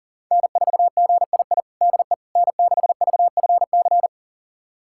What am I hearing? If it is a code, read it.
Morse code, heard as a keyed tone, about 40 wpm, 700 Hz. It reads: N4GII DE N6VFC
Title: USA Callsign DE Callsign - 5 Characters - 40wpm